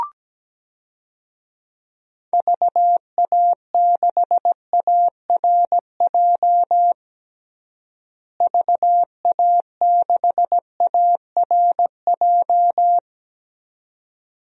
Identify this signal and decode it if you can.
Morse code, heard as a keyed tone, about 17 wpm, 700 Hz. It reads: VA6ARJ VA6ARJ
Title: DXCC-100 Call Signs - 6 Characters - Repeated 2x - 17wpm